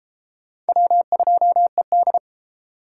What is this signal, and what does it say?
Morse code, heard as a keyed tone, about 29 wpm, 700 Hz. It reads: W2ED